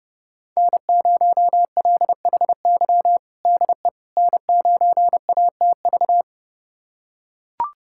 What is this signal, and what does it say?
Morse code, heard as a keyed tone, about 31 wpm, 700 Hz. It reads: N0LHY DE N9ATV
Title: USA Callsign DE Callsign - 5 Characters - 30wpm